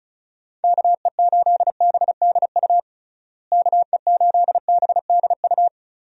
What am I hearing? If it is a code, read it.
Morse code, heard as a keyed tone, about 34 wpm, 700 Hz. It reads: KE8BDU KE8BDU
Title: USA Call Signs - 6 Characters - Repeated 3x - 35wpm